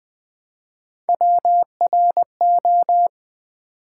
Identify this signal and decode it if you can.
Morse code, heard as a keyed tone, about 20 wpm, 700 Hz. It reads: WRO
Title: Build A Word - 1 Letter At A Time - 3 to 5-Letter Words - 20wpm